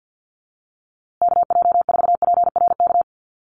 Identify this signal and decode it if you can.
Morse code, heard as a keyed tone, about 50 wpm, 700 Hz. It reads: KJ4PRK
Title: USA Call Signs Repeated 3x 50wpm